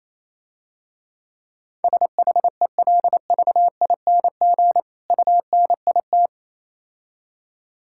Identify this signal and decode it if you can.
Morse code, heard as a keyed tone, about 27 wpm, 700 Hz. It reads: SHELVING UNIT
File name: Easy Words - Speed-Racing with Hints - 22wpm